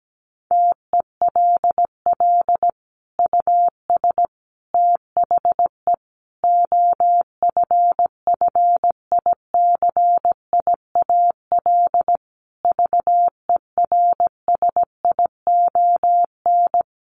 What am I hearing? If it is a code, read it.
Morse code, heard as a keyed tone, about 17 wpm, 700 Hz. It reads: TELL US THE OFFICIAL VERSION
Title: Sentences from Top 1400 Words 17wpm